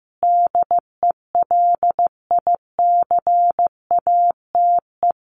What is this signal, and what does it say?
Morse code, heard as a keyed tone, about 15 wpm, 700 Hz. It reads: DELICATE
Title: A Sound Of Thunder 15wpm